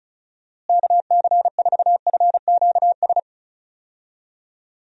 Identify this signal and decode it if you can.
Morse code, heard as a keyed tone, about 34 wpm, 700 Hz. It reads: KC4FQS
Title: USA Call Signs - 6 Characters - Repeated 3x - 35wpm